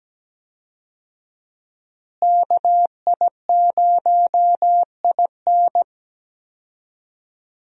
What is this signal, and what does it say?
Morse code, heard as a keyed tone, about 17 wpm, 700 Hz. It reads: KI0IN
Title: USA Call Signs - 5 Characters - Repeated 2x - 17wpm